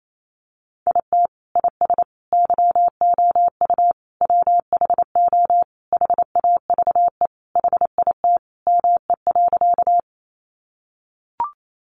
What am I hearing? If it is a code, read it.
Morse code, heard as a keyed tone, about 28 wpm, 700 Hz. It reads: IT IS YOU WHO HAVE HIT ME.
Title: Sentences from Top 1200 Words 28wpm